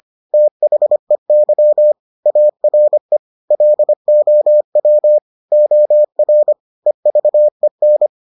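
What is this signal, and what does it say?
Morse code, heard as a keyed tone, about 25 wpm, 600 Hz. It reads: THEY ARE LOW OR EVEN
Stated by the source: Morse edition of the Foundations of Amateur Radio episodes.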